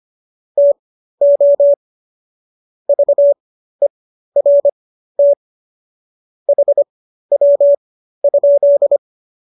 Morse code, heard as a keyed tone, about 25 wpm, spaced as for 13 wpm, 570 Hz. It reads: TO VERT HW?